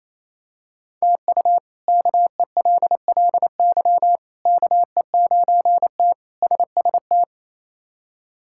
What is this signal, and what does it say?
Morse code, heard as a keyed tone, about 28 wpm, 700 Hz. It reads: TU KELLY KE9T SST
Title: K1USN SST QSO Practice - 28wpm